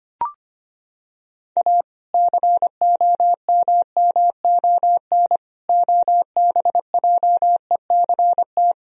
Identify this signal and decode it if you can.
Morse code, heard as a keyed tone, about 25 wpm, 700 Hz. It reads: A COMMON OBJECT
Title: Sentences from Top 1000 Words - Review Part 3 - 25wpm